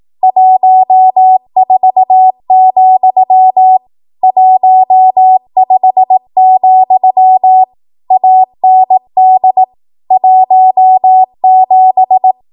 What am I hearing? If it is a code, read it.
Morse code, heard as a keyed tone, about 18 wpm, 750 Hz. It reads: 14, 15, AND 17
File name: Morse / 18WPM